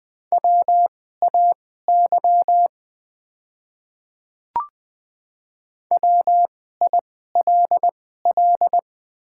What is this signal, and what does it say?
Morse code, heard as a keyed tone, about 20 wpm, spaced as for 15 wpm, 700 Hz. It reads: WAY WILL
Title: 2x Character Spacing - Top 100 Words - 20wpm